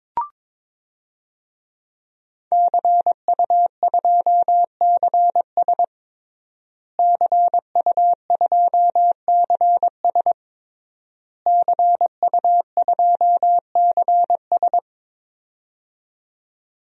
Morse code, heard as a keyed tone, about 22 wpm, 700 Hz. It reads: CU2CS CU2CS CU2CS
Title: DXCC-100 Call Signs - 5 Characters - Repeated 3x - 22wpm